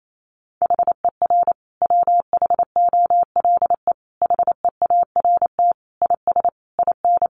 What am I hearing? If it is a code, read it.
Morse code, heard as a keyed tone, about 28 wpm, 700 Hz. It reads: HER WHOLE HEART IS IN
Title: Sentences from Top 1500 Words 28wpm